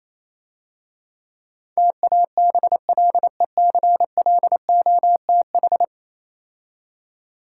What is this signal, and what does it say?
Morse code, heard as a keyed tone, about 28 wpm, 700 Hz. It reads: TABLECLOTH